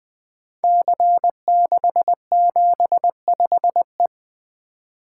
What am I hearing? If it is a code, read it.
Morse code, heard as a keyed tone, about 20 wpm, 700 Hz. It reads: C675E
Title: Code Groups - 5 Letter:Numbers - 20wpm